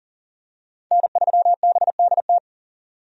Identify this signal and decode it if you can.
Morse code, heard as a keyed tone, about 40 wpm, 700 Hz. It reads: N3BDT
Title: USA Call Signs - 5 Characters - 40wpm